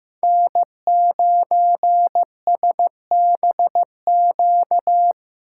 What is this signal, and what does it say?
Morse code, heard as a keyed tone, about 15 wpm, 700 Hz. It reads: N9SBQ
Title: USA Call Signs - 6x Character Spacing - 15wpm